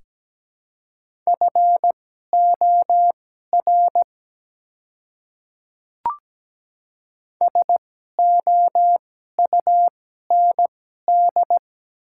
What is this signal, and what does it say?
Morse code, heard as a keyed tone, about 17 wpm, spaced as for 12 wpm, 690 Hz. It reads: FOR SOUND
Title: 2x Character Spacing - Top 300 Words - 17wpm